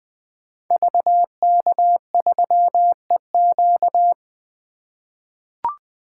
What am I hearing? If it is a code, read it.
Morse code, heard as a keyed tone, about 20 wpm, 700 Hz. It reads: VK3EQ
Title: 2023 SOTA Activator Call Signs - 20wpm